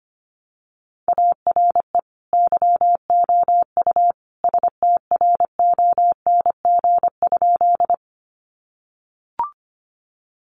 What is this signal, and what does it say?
Morse code, heard as a keyed tone, about 25 wpm, 700 Hz. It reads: ARE YOU STRONG?
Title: Sentences from Top 500 Words 25wpm